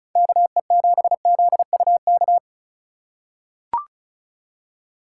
Morse code, heard as a keyed tone, about 36 wpm, 700 Hz. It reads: KE7ZUK
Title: USA Call Signs - 6 Characters - 3x Character Spacing - 35wpm